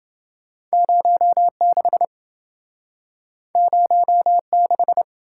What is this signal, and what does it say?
Morse code, heard as a keyed tone, about 27 wpm, 700 Hz. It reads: 06 06